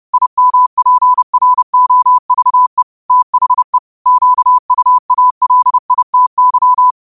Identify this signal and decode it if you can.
Morse code, heard as a keyed tone, about 30 wpm, 1000 Hz. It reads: IMPROVE THE QUALITY
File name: qotdfiles / 30wpm